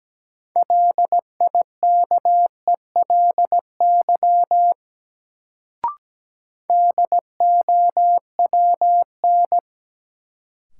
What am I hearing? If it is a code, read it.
Morse code, heard as a keyed tone, about 17 wpm, 700 Hz. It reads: LIKELY DOWN
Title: Sets of 2 Words - New Word Drill - 17wpm